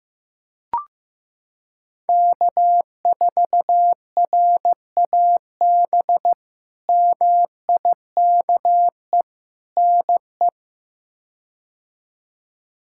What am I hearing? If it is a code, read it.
Morse code, heard as a keyed tone, about 16 wpm, 700 Hz. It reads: K4RAB MIKE NE
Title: K1USN SST QSO Practice - 15wpm